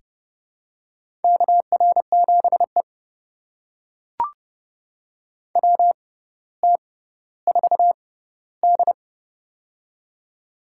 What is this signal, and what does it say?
Morse code, heard as a keyed tone, about 30 wpm, spaced as for 15 wpm, 700 Hz. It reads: KR7E WT4D
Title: USA Call Signs - 4 Characters - 6x Character Spacing - 30wpm